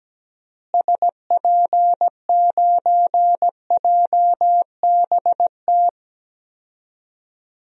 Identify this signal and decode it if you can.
Morse code, heard as a keyed tone, about 17 wpm, 700 Hz. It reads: SP9JBT
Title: DXCC-100 Call Signs - 6 Characters - Repeated 3x - 17wpm